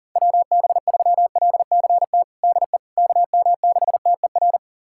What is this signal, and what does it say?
Morse code, heard as a keyed tone, about 37 wpm, 700 Hz. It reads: WB3LCT DE KM6TER
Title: USA Callsign DE Callsign 40wpm